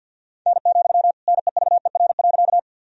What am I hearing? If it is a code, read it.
Morse code, heard as a keyed tone, about 52 wpm, 700 Hz. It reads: N, NEVER.
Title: Sentences from Top 300 Words and Abbreviations 50wpm